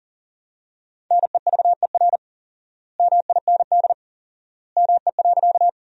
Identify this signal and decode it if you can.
Morse code, heard as a keyed tone, about 37 wpm, 700 Hz. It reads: NEVER MIND ME.
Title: 4x Word-Spacing - Sentences from Top 500 Words - 40wpm